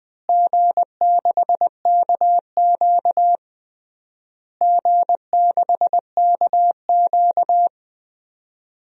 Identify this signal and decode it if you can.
Morse code, heard as a keyed tone, about 20 wpm, 700 Hz. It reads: G6KQ G6KQ
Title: DXCC-100 Call Signs - 4 Characters - Repeated 3x - 20wpm